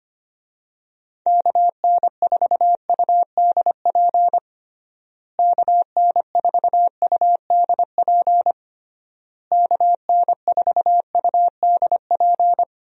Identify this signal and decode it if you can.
Morse code, heard as a keyed tone, about 25 wpm, 700 Hz. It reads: KN4UDP KN4UDP KN4UDP
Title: USA Call Signs - 6 Characters - Repeated 3x - 25wpm